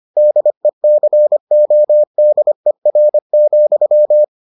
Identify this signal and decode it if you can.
Morse code, heard as a keyed tone, about 25 wpm, 600 Hz. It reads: DECODER,